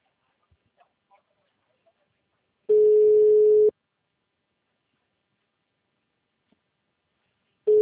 Ringback tone (Europe, 425 Hz).